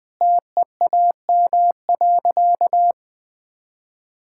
Morse code, heard as a keyed tone, about 21 wpm, 700 Hz. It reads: TEAM.
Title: Sentences with 2-Word Phrases - Spoken Beginning - 20wpm